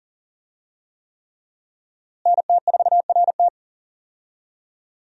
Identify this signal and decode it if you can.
Morse code, heard as a keyed tone, about 39 wpm, 700 Hz. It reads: NT4RT
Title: USA Call Signs - 5 Characters - Repeated 3x - 40wpm